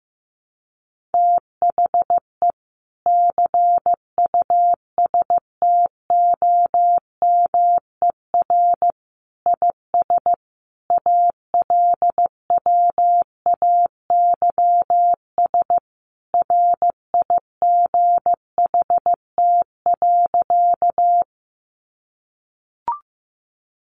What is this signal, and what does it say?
Morse code, heard as a keyed tone, about 15 wpm, 700 Hz. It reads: THE CUSTOMER IS ALWAYS RIGHT.
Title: Sentences from Top 1100 Words 15wpm